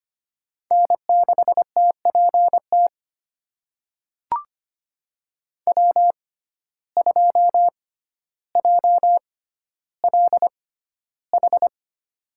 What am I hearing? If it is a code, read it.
Morse code, heard as a keyed tone, about 25 wpm, spaced as for 13 wpm, 700 Hz. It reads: N6TPT W2JLH
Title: USA Call Signs - 5 Characters - 6x Character Spacing - 25wpm